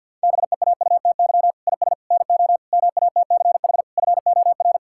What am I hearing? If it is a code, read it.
Morse code, heard as a keyed tone, about 50 wpm, 700 Hz. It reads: BEAUTY IS NO MATCH FOR
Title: Sentences from Top 1600 Words 50wpm